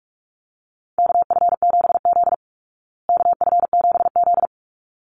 Morse code, heard as a keyed tone, about 44 wpm, 700 Hz. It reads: KF7Z KF7Z
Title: Call Signs Repeated 3x - 45wpm